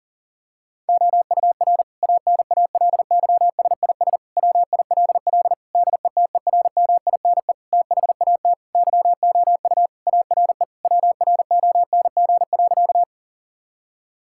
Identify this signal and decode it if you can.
Morse code, heard as a keyed tone, about 37 wpm, 700 Hz. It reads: OUR ANALYSIS WILL DETERMINE THAT YOU ARE WRONG.
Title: Sentences from Top 1000 Words - Review Part 5 - 40wpm